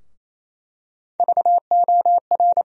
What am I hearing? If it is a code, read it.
Morse code, heard as a keyed tone, about 28 wpm, 700 Hz. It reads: VOR
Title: USA Call Sign Postfixes - 3 Letters 28wpm